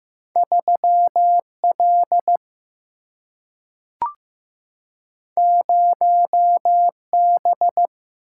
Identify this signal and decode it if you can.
Morse code, heard as a keyed tone, about 15 wpm, 700 Hz. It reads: 3 L 0 B
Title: Single-Letter Single-Number 15wpm